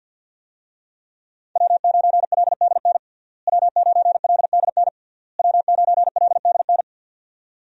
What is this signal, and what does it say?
Morse code, heard as a keyed tone, about 53 wpm, 690 Hz. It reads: W9LDN W9LDN W9LDN
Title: USA Call Signs - 5 Characters - Repeated 3x - 50wpm